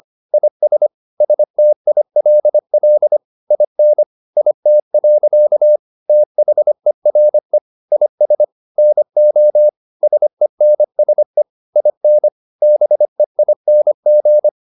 Morse code, heard as a keyed tone, about 25 wpm, 600 Hz. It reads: IS STILL IN IT. THERE IS NO SENSE IN BEING
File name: the old man and the sea part 1 25wpm